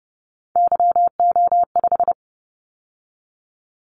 Morse code, heard as a keyed tone, about 30 wpm, 700 Hz. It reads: YO5
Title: Callsign Prefixes for Top 25 DXCC - 30wpm